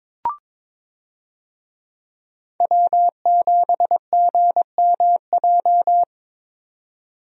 Morse code, heard as a keyed tone, about 22 wpm, 700 Hz. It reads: W7GMJ
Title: USA Call Signs - 5 Characters - 22wpm